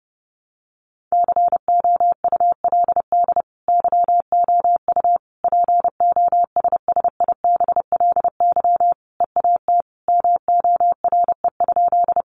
Morse code, heard as a keyed tone, about 30 wpm, 700 Hz. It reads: COULD YOU POSSIBLY EAT MORE?
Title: Sentences from Top 1300 Words 30wpm